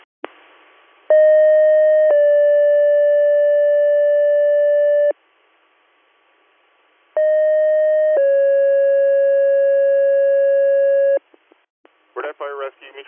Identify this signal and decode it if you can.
Two-tone page, heard as a one-second tone, then a three-second tone, about 620 then 580 Hz, about 620 then 550 Hz, with speech. It is heard from 1.1 to 11.2 s.
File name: 100- .0-call 403